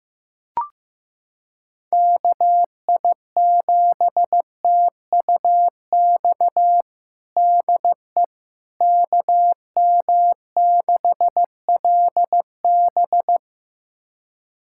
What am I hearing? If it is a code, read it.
Morse code, heard as a keyed tone, about 15 wpm, 700 Hz. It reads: KI7TUX DE KM6LB
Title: USA Callsign DE Callsign 15wpm